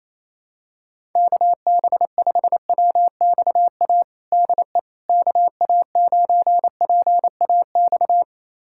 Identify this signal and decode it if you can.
Morse code, heard as a keyed tone, about 28 wpm, 700 Hz. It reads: KB5WXA DE KA9PAX
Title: USA Callsign DE Callsign - 6 Characters - 28wpm